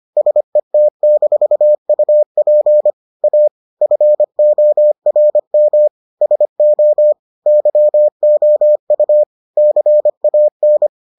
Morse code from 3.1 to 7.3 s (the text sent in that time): A FORM SO